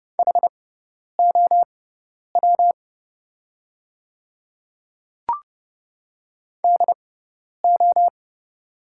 Morse code, heard as a keyed tone, about 31 wpm, spaced as for 10 wpm, 700 Hz. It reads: HOW DO
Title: 6x Character Spacing - Top 100 Words - 30wpm